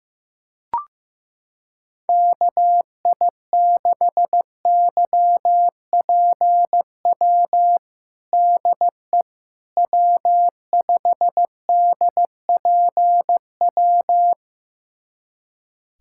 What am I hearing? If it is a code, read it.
Morse code, heard as a keyed tone, about 15 wpm, 700 Hz. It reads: KI6YPW DE W5DPW
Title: USA Callsign DE Callsign 15wpm